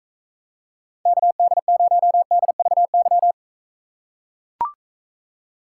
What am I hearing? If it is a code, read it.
Morse code, heard as a keyed tone, about 43 wpm, 700 Hz. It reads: KD0DVY